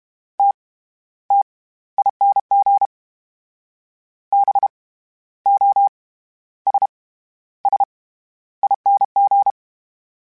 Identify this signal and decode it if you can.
Morse code, heard as a keyed tone, about 32 wpm, 800 Hz. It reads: T T ING B O S S ING
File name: Special INGs / Special INGs-mp3s / SlowToFaster ing @30